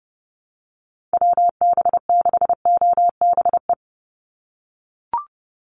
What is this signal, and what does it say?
Morse code, heard as a keyed tone, about 30 wpm, 700 Hz. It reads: WB6OBE
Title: USA Call Signs 30wpm